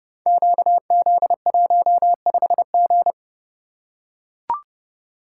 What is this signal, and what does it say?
Morse code, heard as a keyed tone, about 30 wpm, 700 Hz. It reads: QZ15G
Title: Code Groups - 5 Letter:Numbers - 30wpm